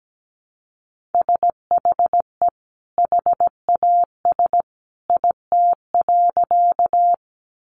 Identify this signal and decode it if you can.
Morse code, heard as a keyed tone, about 17 wpm, 700 Hz. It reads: SHE HAS IT.
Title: Sentences from Top 100 Words 17wpm